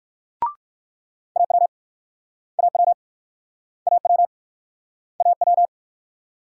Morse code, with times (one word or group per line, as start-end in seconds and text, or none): 1.4-1.7 AW
2.6-2.9 AW
3.9-4.3 AW
5.2-5.7 AW